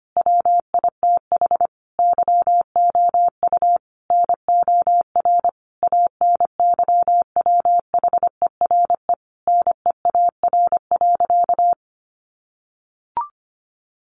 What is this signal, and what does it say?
Morse code, heard as a keyed tone, about 25 wpm, 700 Hz. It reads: WITH YOU NOR ANYWHERE NEAR.
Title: Sentences from Top 1500 Words 25wpm